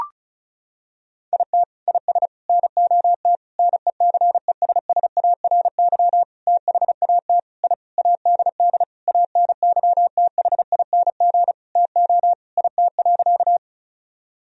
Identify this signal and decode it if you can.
Morse code, heard as a keyed tone, about 36 wpm, 700 Hz. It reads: IT IS NOT NECESSARY THAT I ADD ANYTHING TO IT.
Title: Sentences from Top 1000 Words - Review Part 1 - 35wpm